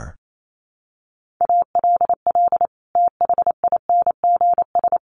Morse code, heard as a keyed tone, about 28 wpm, 700 Hz. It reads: ALL THINGS
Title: Sets of 5 Words Encore 28wpm